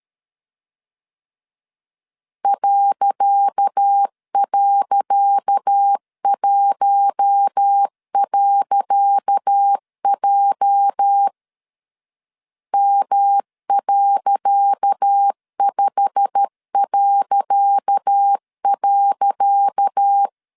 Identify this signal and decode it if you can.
Morse code, heard as a keyed tone, about 13 wpm, 800 Hz. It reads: ..1.J M.5..
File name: LEZIONI / 10 SIGNS / 10C DOT 12wpm 2451